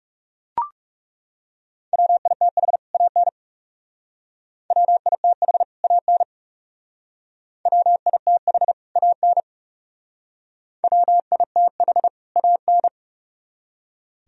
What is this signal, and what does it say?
Morse code, heard as a keyed tone, about 36 wpm, 700 Hz. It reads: WITH AN WITH AN WITH AN WITH AN
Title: Speed-Racing - Sets of 2 Words 30wpm